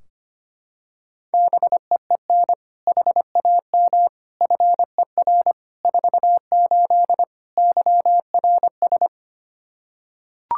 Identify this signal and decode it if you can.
Morse code, heard as a keyed tone, about 25 wpm, 700 Hz. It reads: BEEN HAM FER 48 YRS
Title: Practice QSO - Exchange 3 - More Info - 25wpm